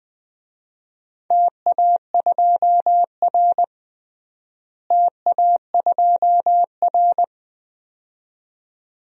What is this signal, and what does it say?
Morse code, heard as a keyed tone, about 20 wpm, 700 Hz. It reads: TA2R TA2R
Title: DXCC-100 Call Signs - 4 Characters - Repeated 2x - 20wpm